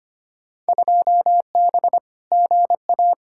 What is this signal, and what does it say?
Morse code, heard as a keyed tone, about 25 wpm, 700 Hz. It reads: 2B GA
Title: Field Day Exchanges Encore - 25wpm